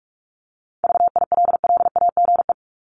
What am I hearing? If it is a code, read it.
Morse code, heard as a keyed tone, about 44 wpm, 700 Hz. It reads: VILLAGE
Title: Top 1751-2000 Words 45wpm